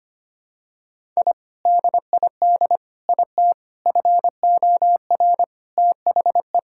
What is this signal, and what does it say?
Morse code, heard as a keyed tone, about 25 wpm, 700 Hz. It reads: I DID IT FOR THE
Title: Sentences from Top 1000 Words - Review Part 6 - 25wpm